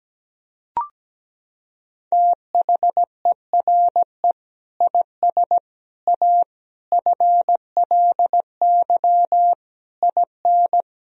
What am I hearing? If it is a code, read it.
Morse code, heard as a keyed tone, about 17 wpm, 700 Hz. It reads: THERE IS A FLY IN
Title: Sentences from Top 1000 Words - Review Part 4 - 17wpm